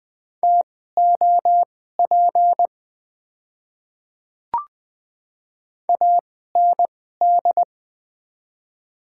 Morse code, heard as a keyed tone, about 20 wpm, spaced as for 14 wpm, 700 Hz. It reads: TOP AND
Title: 2x Character Spacing - Top 500 Words - 20wpm